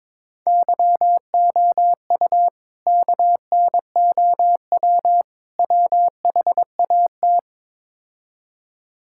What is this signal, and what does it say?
Morse code, heard as a keyed tone, about 22 wpm, 700 Hz. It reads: YOU KNOW WHAT
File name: practice / 22wpm / Build A Phrase - 5-Word Phrases - 22wpm